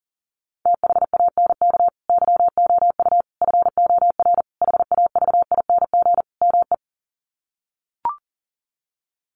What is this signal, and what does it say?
Morse code, heard as a keyed tone, about 40 wpm, 700 Hz. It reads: THANK YOU FOR HAVING ME
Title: Sets of 5 Words Encore 40wpm